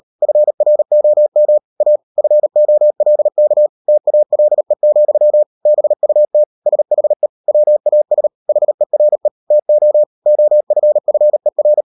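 Morse code, heard as a keyed tone, about 38 wpm, 600 Hz. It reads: FROM A FOLK TALE, BUT SHE WAS HERE TO OFFER
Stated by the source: Morse code practice at 38 WPM, sending Ken Follett's novel Never in English, from CW med Gnister.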